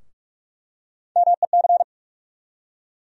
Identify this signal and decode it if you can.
Morse code, heard as a keyed tone, about 44 wpm, 700 Hz. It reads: MEC
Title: USA Call Sign Postfixes - 3 Letters 45wpm